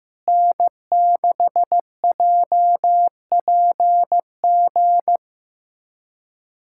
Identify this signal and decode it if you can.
Morse code, heard as a keyed tone, about 15 wpm, 700 Hz. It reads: N6JPG
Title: USA Callsign DE Callsign - 5 Characters - 15wpm